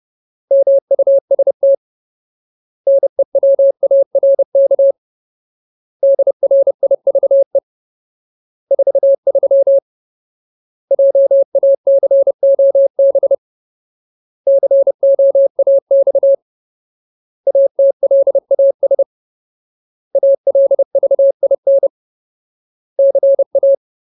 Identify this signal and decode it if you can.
Morse code, heard as a keyed tone, about 30 wpm, 550 Hz. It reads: MUST NEWARK DRIVE 43 JACOB COAX ATLAS ALVIN CA
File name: Ultimate Word List / 30WPM Ultimate